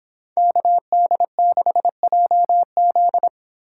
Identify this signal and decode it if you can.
Morse code, heard as a keyed tone, about 25 wpm, 700 Hz. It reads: KD6JZ
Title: Speed Racing - USA 2x2 Call Signs 22wpm